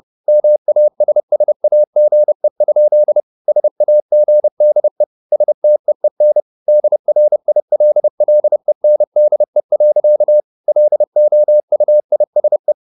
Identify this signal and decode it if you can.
Morse code, heard as a keyed tone, about 30 wpm, 600 Hz. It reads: MASSAGE? SAGDE STEEN DRILLENDE. LOUISE
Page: Ferie 30 wpm